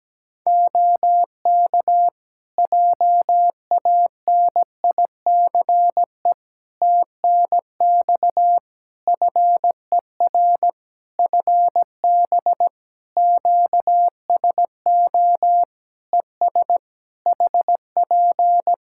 Morse code, heard as a keyed tone, about 17 wpm, 700 Hz. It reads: OK JANICE TNX FER FB QSO ES HP
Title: K1USN SST QSO Practice 17wpm